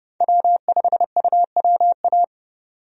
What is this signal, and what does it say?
Morse code, heard as a keyed tone, about 30 wpm, 700 Hz. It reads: W5UWA
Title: USA Call Signs - 5 Characters - 30wpm